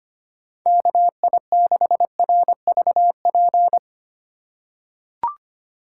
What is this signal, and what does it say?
Morse code, heard as a keyed tone, about 25 wpm, 700 Hz. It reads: KI6RVP
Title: USA Call Signs - 6 Characters - 4x Character Spacing - 25wpm